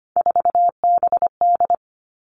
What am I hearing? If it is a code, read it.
Morse code, heard as a keyed tone, about 25 wpm, 700 Hz. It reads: 4BD
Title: Three Challenging Letter-Number Combinations 25wpm